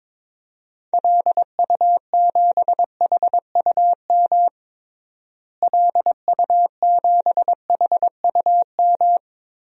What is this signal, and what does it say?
Morse code, heard as a keyed tone, about 22 wpm, 700 Hz. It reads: LU7HUM LU7HUM
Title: DXCC-100 Call Signs - 6 Characters - Repeated 2x - 22wpm